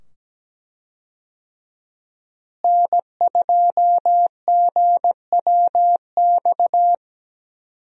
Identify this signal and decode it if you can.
Morse code, heard as a keyed tone, about 17 wpm, 700 Hz. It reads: N2GWX
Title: USA Call Signs - 5 Characters - Repeated 2x - 17wpm